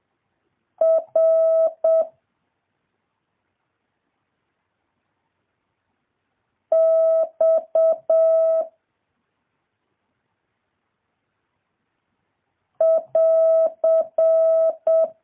Morse code, heard as a keyed tone, about 7 wpm, 650 Hz. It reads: R X <AR>
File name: jh8cbh / 2023cw